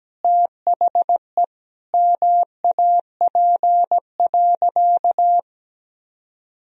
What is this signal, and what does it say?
Morse code, heard as a keyed tone, about 17 wpm, 700 Hz. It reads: THE MAP.
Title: Sentences from Top 1000 Words - Review Part 2 - 17wpm